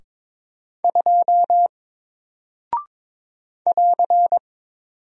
Morse code, heard as a keyed tone, about 22 wpm, 700 Hz. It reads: 2 <AR>
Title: Single Character (Letter:Number:Punctuation:Prosign) Alt 22wpm